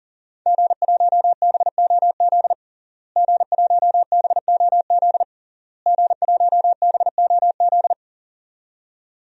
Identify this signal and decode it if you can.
Morse code, heard as a keyed tone, about 37 wpm, 700 Hz. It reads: G1BOZ G1BOZ G1BOZ
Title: DXCC-100 Call Signs - 5 Characters - Repeated 3x - 40wpm